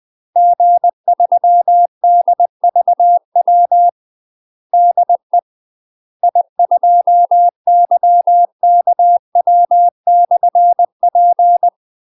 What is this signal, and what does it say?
Morse code, heard as a keyed tone, about 20 wpm, 700 Hz. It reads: G3DVW DE I2YKW/P
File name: UK MASTER QSOs TRAINING / 20WPM / UK MASTER QSOs 0037